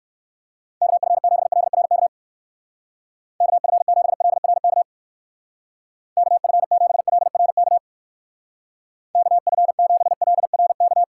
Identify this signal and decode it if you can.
Morse code, heard as a keyed tone, about 56 wpm, 700 Hz. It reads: KF7LRK KF7LRK KF7LRK KF7LRK
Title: Speed Racing - USA 2x3 Call Signs 45wpm